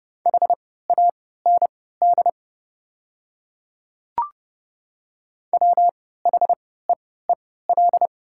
Morse code, heard as a keyed tone, about 30 wpm, spaced as for 17 wpm, 690 Hz. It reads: HAND WHEEL
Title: 3x Character Spacing - Top 500 Words - 30wpm